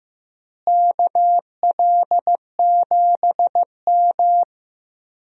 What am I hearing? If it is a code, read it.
Morse code, heard as a keyed tone, about 15 wpm, 700 Hz. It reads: KL7M
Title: DXCC-100 Call Signs - 4 Characters - Repeated 2x - 15wpm